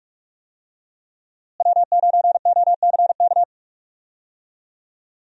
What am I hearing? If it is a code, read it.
Morse code, heard as a keyed tone, about 46 wpm, 700 Hz. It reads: W9OCK
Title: USA Call Signs - 5 Characters - Repeated 2x - 45wpm